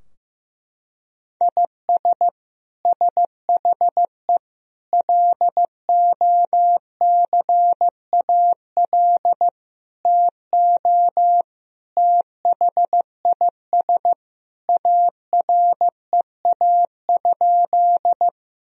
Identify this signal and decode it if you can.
Morse code, heard as a keyed tone, about 15 wpm, 700 Hz. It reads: IS SHE LOCAL TO THIS AREA?
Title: Sentences from Top 1000 Words - Review Part 3 - 15wpm